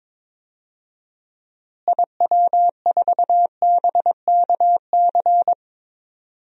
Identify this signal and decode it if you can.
Morse code, heard as a keyed tone, about 22 wpm, 700 Hz. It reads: IW4BKC
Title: DXCC-100 Call Signs - 6 Characters - Repeated 2x - 22wpm